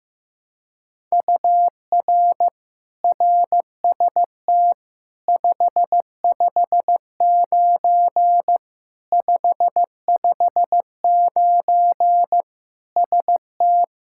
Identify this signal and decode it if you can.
Morse code, heard as a keyed tone, about 15 wpm, 700 Hz. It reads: UR RST 559 559 ST
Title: Practice QSO - Exchange 1 - Beginning - 15wpm